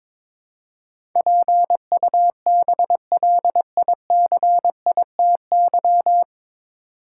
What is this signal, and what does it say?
Morse code, heard as a keyed tone, about 22 wpm, 700 Hz. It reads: PUBLICITY
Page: Morse Code Ninja practice 22wpm